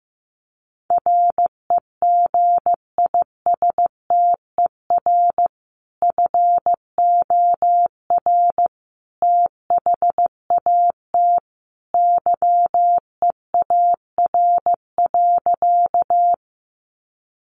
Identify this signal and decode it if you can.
Morse code, heard as a keyed tone, about 15 wpm, 700 Hz. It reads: REGISTER FOR THAT YEAR.
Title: Sentences from Top 1200 Words 15wpm